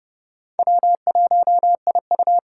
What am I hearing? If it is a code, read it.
Morse code, heard as a keyed tone, about 40 wpm, 700 Hz. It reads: W1IU